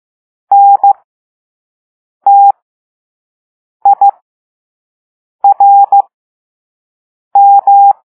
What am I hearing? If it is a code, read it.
Morse code, heard as a keyed tone, about 15 wpm, spaced as for 5 wpm, 800 Hz. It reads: NTIRM
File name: Morse / Level3-All-B